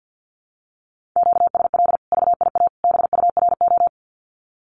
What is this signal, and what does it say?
Morse code, heard as a keyed tone, about 50 wpm, 700 Hz. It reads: QSL VIA BURO
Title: Common QSO Phrases 50wpm